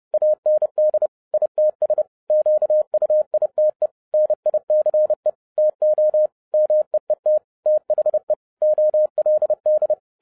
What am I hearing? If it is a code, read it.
Morse code, heard as a keyed tone, about 30 wpm, 610 Hz. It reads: AND ITS QUITE NICE TO MEET THE OLD
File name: 2012-01-mp3 / n12-I DIDNT MISS THE SOLAR CYCLE 24-30wpm